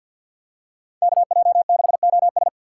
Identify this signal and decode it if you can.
Morse code, heard as a keyed tone, about 49 wpm, 700 Hz. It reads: KJ6OS
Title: USA Call Signs - 2x Character Spacing - 50wpm